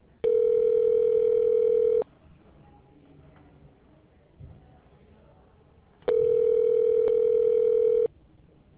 Ringback tone (North America), 440 + 480 Hz.